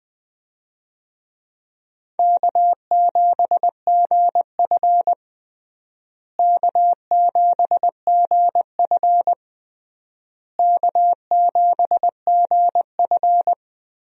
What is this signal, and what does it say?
Morse code, heard as a keyed tone, about 20 wpm, 700 Hz. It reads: K7GF K7GF K7GF
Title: USA Call Signs - 4 Characters - Repeated 3x - 20wpm